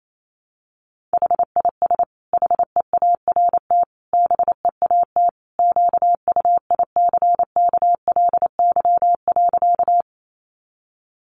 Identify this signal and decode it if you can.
Morse code, heard as a keyed tone, about 28 wpm, 700 Hz. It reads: HIS HEART BEAT QUICKLY.
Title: Sentences from Top 1300 Words 28wpm